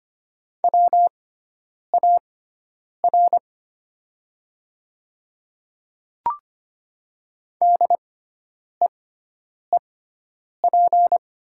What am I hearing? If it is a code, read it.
Morse code, heard as a keyed tone, about 25 wpm, spaced as for 9 wpm, 700 Hz. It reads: WAR DEEP
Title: 6x Character Spacing - Top 500 Words - 25wpm